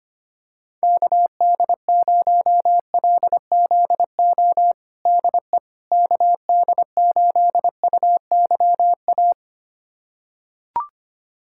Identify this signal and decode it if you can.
Morse code, heard as a keyed tone, about 25 wpm, 700 Hz. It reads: KD0LZO DE KD8UYA
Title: USA Callsign DE Callsign - 6 Characters - 25wpm